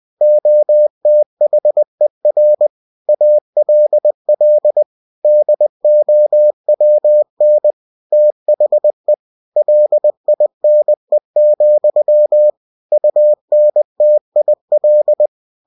Morse code, heard as a keyed tone, about 20 wpm, 600 Hz. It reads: OTHER ALL DOWN THE LINE, UNTIL